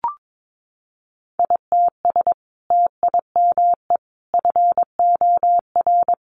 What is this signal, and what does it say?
Morse code, heard as a keyed tone, about 22 wpm, 700 Hz. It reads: ITS TIME FOR
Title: Sentences from Top 600 Words 22wpm